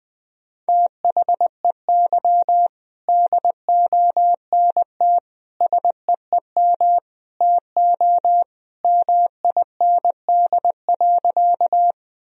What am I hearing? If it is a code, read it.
Morse code, heard as a keyed tone, about 20 wpm, 700 Hz. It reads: THEY DONT SEEM TO MIND.
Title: Sentences from Top 1000 Words - Review Part 1 - 20wpm